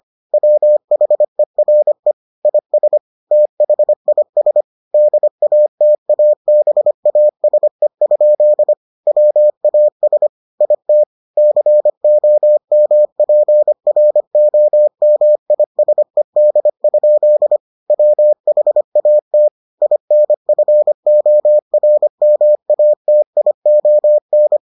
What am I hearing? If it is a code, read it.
Morse code, heard as a keyed tone, about 25 wpm, 600 Hz. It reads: WHERE IS THIS DATABASE? WAS IT COMPROMISED? WHAT INFORMATION